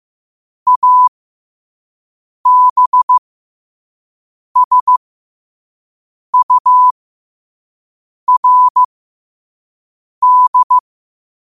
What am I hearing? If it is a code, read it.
Morse code, heard as a keyed tone, about 15 wpm, spaced as for 5 wpm, 1000 Hz. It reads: ABSURD
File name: qotdfiles / 5wpm